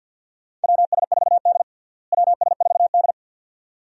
Morse code, heard as a keyed tone, about 52 wpm, 690 Hz. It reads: WS4D WS4D
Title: USA Call Signs - 4 Characters - Repeated 2x - 50wpm